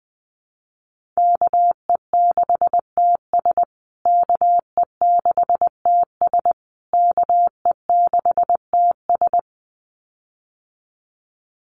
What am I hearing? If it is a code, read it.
Morse code, heard as a keyed tone, about 20 wpm, 700 Hz. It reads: KE6TS KE6TS KE6TS
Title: USA Call Signs Repeated 3x 20wpm